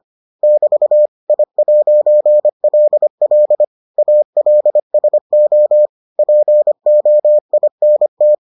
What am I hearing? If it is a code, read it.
Morse code, heard as a keyed tone, about 25 wpm, 600 Hz. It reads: <BT> I'LL ALSO POINT